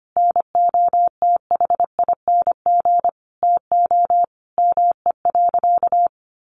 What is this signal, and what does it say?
Morse code, heard as a keyed tone, about 25 wpm, 700 Hz. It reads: NOTHING TO ME.
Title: Sentences from Top 1400 Words 25wpm